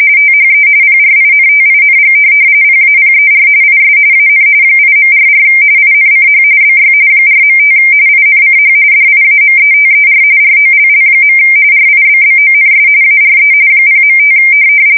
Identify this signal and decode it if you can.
RTTY, about 45.45 baud. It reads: THE PRINTING PAPER IS AN 8.44 INCH BY 4.5 INCH DIAMETER ROLL. RIBBONS ARE 0.5
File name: fsk170 / teletype-corp